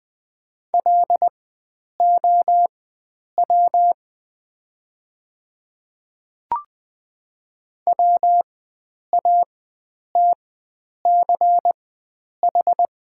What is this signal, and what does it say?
Morse code, heard as a keyed tone, about 20 wpm, spaced as for 9 wpm, 700 Hz. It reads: LOW WATCH
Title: 4x Character Spacing - Top 300 Words - 20wpm